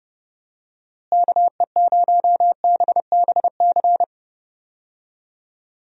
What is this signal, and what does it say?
Morse code, heard as a keyed tone, about 30 wpm, 700 Hz. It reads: KE0BBC
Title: USA Call Signs - 6 Characters - Repeated 2x - 30wpm